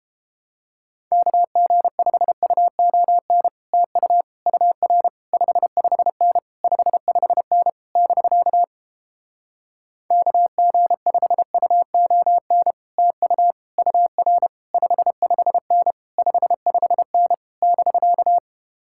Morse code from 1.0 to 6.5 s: KG5UON TU UR 55N

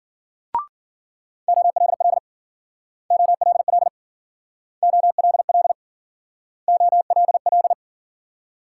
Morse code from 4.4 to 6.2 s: OLL

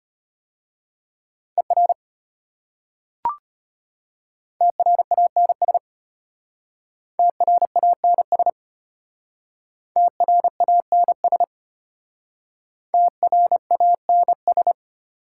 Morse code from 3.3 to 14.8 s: TRANS TRANS TRANS TRANS

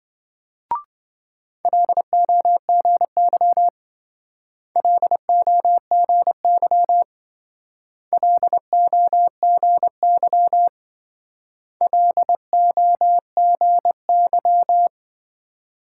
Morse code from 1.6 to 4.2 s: LOGY

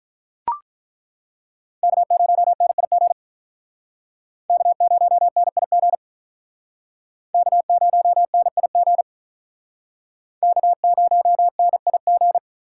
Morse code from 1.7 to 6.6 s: K0NIG K0NIG

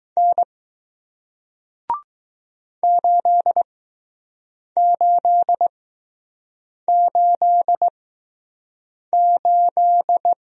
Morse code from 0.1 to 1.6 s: N

